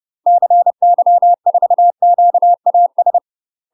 Morse code, heard as a keyed tone, about 30 wpm, 700 Hz. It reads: CY4QAS
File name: UK MASTER QSOs TRAINING / 30WPM / UK MASTER QSOs 0048